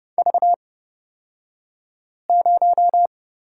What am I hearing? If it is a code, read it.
Morse code, heard as a keyed tone, about 31 wpm, spaced as for 5 wpm, 700 Hz. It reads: V0
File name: Lesson 36d - BT - Two Characters 30wpm